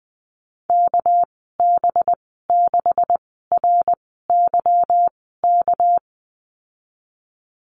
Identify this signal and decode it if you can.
Morse code, heard as a keyed tone, about 20 wpm, spaced as for 15 wpm, 700 Hz. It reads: KB6RYK
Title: 200 Lesson 32g - Letter k - Call Signs 20wpm